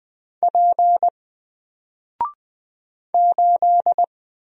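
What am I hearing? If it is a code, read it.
Morse code, heard as a keyed tone, about 20 wpm, 700 Hz. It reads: P 8
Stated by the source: Single letter and number Morse drill at 20 wpm, from N4LCD.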